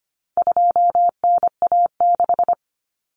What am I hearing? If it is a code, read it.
Morse code, heard as a keyed tone, about 25 wpm, 700 Hz. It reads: 2NA6
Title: Common Cut Numbers - 4-Digit 25wpm